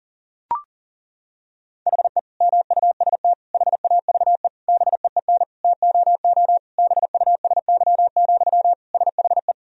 Morse code, heard as a keyed tone, about 37 wpm, 700 Hz. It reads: HE MUST HAVE BEEN TOO BUSY, SHE
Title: Sentences from Top 1000 Words - Review Part 3 - 40wpm